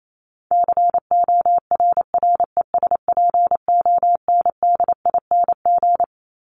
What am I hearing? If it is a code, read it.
Morse code, heard as a keyed tone, about 28 wpm, 700 Hz. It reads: CORRESPONDING
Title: ING Suffix - Top 100 Words with Base Words 28wpm